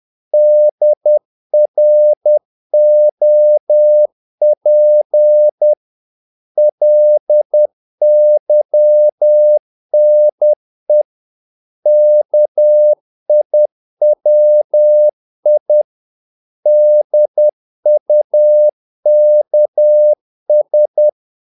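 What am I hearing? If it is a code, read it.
Morse code, heard as a keyed tone, about 10 wpm, 600 Hz. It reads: DROP LYNE KIWI DUKS
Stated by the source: Morse practice of short Danish words at 10 wpm, from CW med Gnister.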